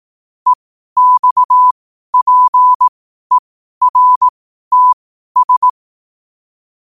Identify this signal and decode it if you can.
Morse code, heard as a keyed tone, about 18 wpm, spaced as for 12 wpm, 1000 Hz. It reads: EXPERTS
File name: qotdfiles / 12wpm